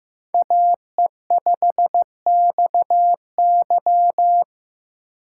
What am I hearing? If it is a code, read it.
Morse code, heard as a keyed tone, about 15 wpm, 700 Hz. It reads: AE5XY
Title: USA Call Signs - 5 Characters - 2x Character Spacing - 15wpm